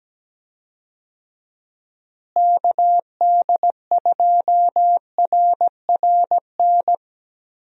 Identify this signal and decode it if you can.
Morse code, heard as a keyed tone, about 17 wpm, 700 Hz. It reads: KD2RRN
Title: USA Call Signs - 6 Characters - Repeated 3x - 17wpm